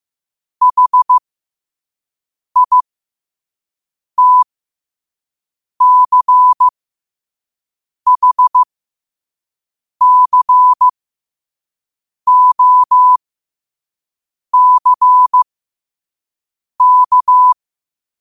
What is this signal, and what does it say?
Morse code, heard as a keyed tone, about 15 wpm, spaced as for 5 wpm, 1000 Hz. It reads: HITCHCOCK
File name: qotdfiles / 5wpm